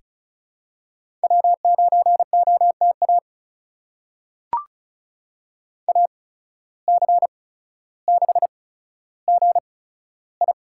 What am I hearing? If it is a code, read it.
Morse code, heard as a keyed tone, about 34 wpm, spaced as for 15 wpm, 700 Hz. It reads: W9OTA AC6GI
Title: USA Call Signs - 5 Characters - 8x Character Spacing - 35wpm